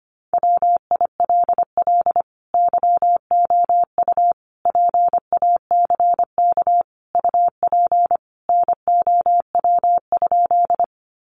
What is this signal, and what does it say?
Morse code, heard as a keyed tone, about 25 wpm, 700 Hz. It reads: WILL YOU PACK UP NOW?
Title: Sentences from Top 1300 Words 25wpm